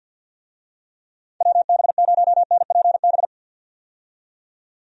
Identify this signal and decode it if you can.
Morse code, heard as a keyed tone, about 47 wpm, 700 Hz. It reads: WB0NPB
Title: USA Call Signs - 6 Characters - 50wpm